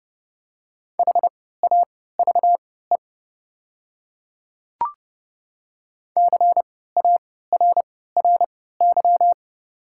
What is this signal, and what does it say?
Morse code, heard as a keyed tone, about 30 wpm, spaced as for 17 wpm, 700 Hz. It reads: HAVE CARRY
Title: 3x Character Spacing - Top 500 Words - 30wpm